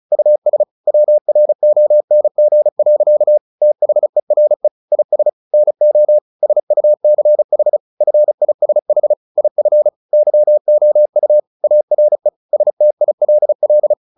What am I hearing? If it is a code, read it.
Morse code, heard as a keyed tone, about 34 wpm, 600 Hz. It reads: US WRONG. THERE IS NO SUCH FISH IF YOU ARE STILL
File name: the old man and the sea / the old man and the sea 1 35wpm